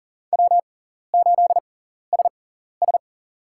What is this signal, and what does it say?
Morse code, heard as a keyed tone, about 40 wpm, spaced as for 14 wpm, 700 Hz. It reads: W8SS
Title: USA Call Signs - 4 Characters - 6x Character Spacing - 40wpm